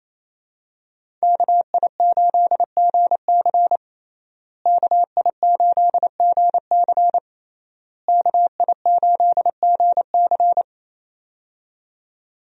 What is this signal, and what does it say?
Morse code, heard as a keyed tone, about 28 wpm, 700 Hz. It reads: KI8GC KI8GC KI8GC
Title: USA Call Signs - 5 Characters - Repeated 3x - 28wpm